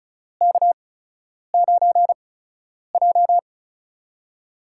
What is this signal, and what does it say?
Morse code, heard as a keyed tone, about 36 wpm, spaced as for 10 wpm, 700 Hz. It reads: K9J
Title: USA Call Signs - 4 Characters - 8x Character Spacing - 35wpm